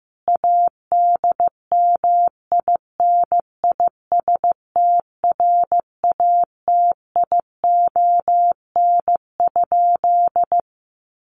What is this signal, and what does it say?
Morse code, heard as a keyed tone, about 15 wpm, 700 Hz. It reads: ADMINISTRATION?
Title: Sentences from Top 1300 Words 15wpm